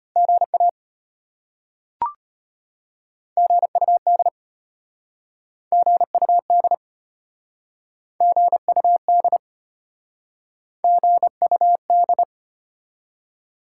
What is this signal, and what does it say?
Morse code, heard as a keyed tone, about 33 wpm, 700 Hz. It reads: GA GUD GUD GUD GUD